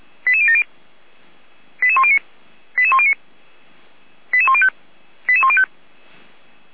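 Five-tone selective call (ZVEI), heard as quick runs of five tones, tones of about 70 ms. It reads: 88070 88109 88109 88105 88105
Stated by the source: ZVEI five-tone selective call received on SDR.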